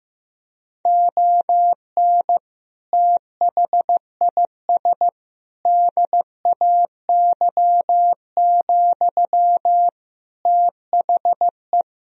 Morse code, heard as a keyed tone, about 15 wpm, 700 Hz. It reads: ON THIS DAY, THE
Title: Sentences from Top 1000 Words - Review Part 2 - 15wpm